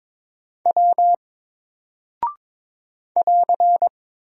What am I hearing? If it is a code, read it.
Morse code, heard as a keyed tone, about 22 wpm, 700 Hz. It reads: W <AR>
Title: Single Character (Letter:Number:Punctuation:Prosign) Alt 22wpm